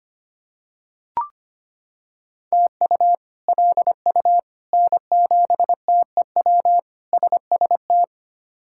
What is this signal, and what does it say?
Morse code, heard as a keyed tone, about 25 wpm, 700 Hz. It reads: TU LU N7TEW SST
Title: K1USN SST QSO Practice - 25wpm